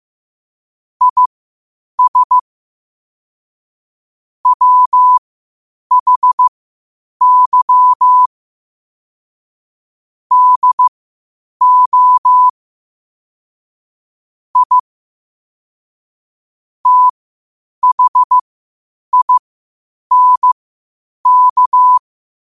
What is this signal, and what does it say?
Morse code, heard as a keyed tone, about 15 wpm, spaced as for 8 wpm, 1000 Hz. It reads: IS WHY DO I THINK
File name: qotdfiles / 8wpm